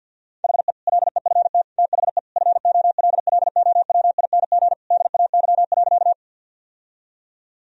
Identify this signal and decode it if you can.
Morse code, heard as a keyed tone, about 51 wpm, 700 Hz. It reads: HE LEFT THE FOLLOWING DAY.
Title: Sentences from Top 1000 Words - Review Part 4 - 50wpm